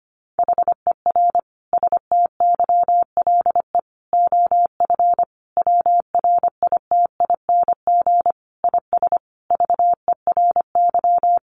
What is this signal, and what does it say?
Morse code, heard as a keyed tone, about 25 wpm, 700 Hz. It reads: HER STYLE OF WRITING IS VERY
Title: Sentences from Top 800 Words 25wpm